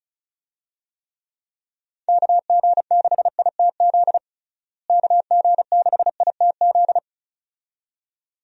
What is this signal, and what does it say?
Morse code, heard as a keyed tone, about 34 wpm, 700 Hz. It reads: KG6ITZ KG6ITZ
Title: USA Call Signs - 6 Characters - Repeated 2x - 35wpm